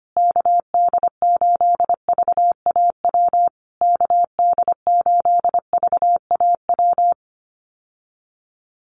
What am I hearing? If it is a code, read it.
Morse code, heard as a keyed tone, about 25 wpm, 700 Hz. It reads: KD8VAW KD8VAW
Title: USA Call Signs Repeated 3x 25wpm